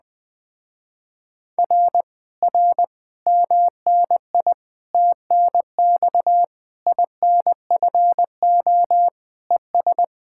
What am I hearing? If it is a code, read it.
Morse code, heard as a keyed tone, about 20 wpm, 700 Hz. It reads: R R MNI TNX INFO ES
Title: Speed-Racing Common QSO Phrases - 17wpm